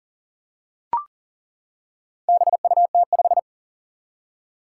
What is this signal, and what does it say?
Morse code, heard as a keyed tone, about 37 wpm, 700 Hz. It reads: BUT5
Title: Four Challenging Letter-Number Combinations 40wpm